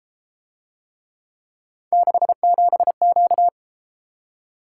Morse code, heard as a keyed tone, about 30 wpm, 700 Hz. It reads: 67Q